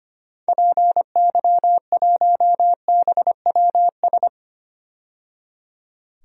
Morse code, heard as a keyed tone, about 25 wpm, 700 Hz. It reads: PY1BWS
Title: DXCC-100 Call Signs - 6 Characters - 25wpm